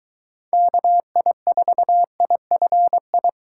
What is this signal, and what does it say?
Morse code, heard as a keyed tone, about 22 wpm, 700 Hz. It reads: KI4IFI